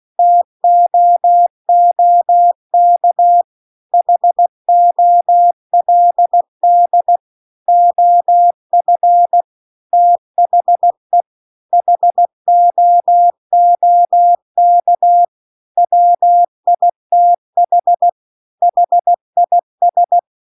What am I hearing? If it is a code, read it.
Morse code, heard as a keyed tone, about 16 wpm, 700 Hz. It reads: TOOK HOLD OF THE HOOK WITH HIS
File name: CW-The-Big-Twoheaded-River-2-16WPM